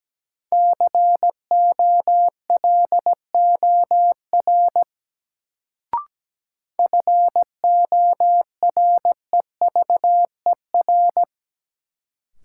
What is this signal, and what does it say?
Morse code, heard as a keyed tone, about 17 wpm, 700 Hz. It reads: COLOR FOREVER
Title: Sentences from Top 1600 Words - New Word Drill Alt - 17wpm